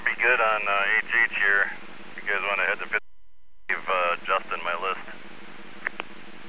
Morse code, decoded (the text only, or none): none